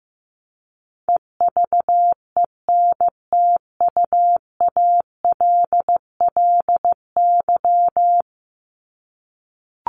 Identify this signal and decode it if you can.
Morse code, heard as a keyed tone, about 15 wpm, 700 Hz. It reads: EVENTUALLY
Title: LY Suffix - Top 100 Words with Base Words 15wpm